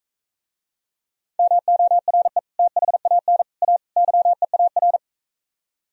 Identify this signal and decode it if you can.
Morse code, heard as a keyed tone, about 43 wpm, 690 Hz. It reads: MORE THAN A YEAR